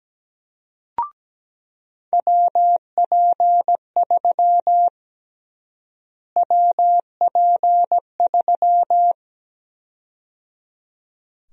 Morse code, heard as a keyed tone, about 17 wpm, 700 Hz. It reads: WP3 WP3
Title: USA Call Sign Prefixes - Repeated 2x - 17wpm